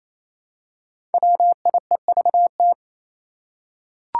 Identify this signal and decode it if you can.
Morse code, heard as a keyed tone, about 28 wpm, 700 Hz. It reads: WIEVT
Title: Five Challenging Letter-Number Combinations 28wpm